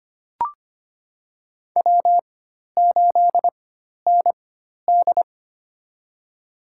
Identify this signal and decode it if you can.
Morse code, heard as a keyed tone, about 25 wpm, spaced as for 12 wpm, 700 Hz. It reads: W8ND
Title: USA Call Signs - 4x Character Spacing - 25wpm